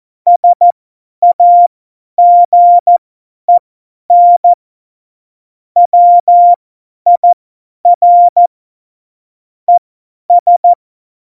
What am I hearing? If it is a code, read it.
Morse code, heard as a keyed tone, about 14 wpm, spaced as for 10 wpm, 700 Hz. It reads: SAGEN WIR ES
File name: Morsekurs / Witze / Witz 26. Februar 2019, 50 F70